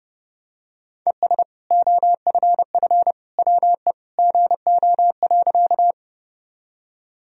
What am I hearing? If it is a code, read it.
Morse code, heard as a keyed tone, about 32 wpm, 700 Hz. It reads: ES OFF WE GO.
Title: Sentences from Top 200 Words and Abbreviations 30wpm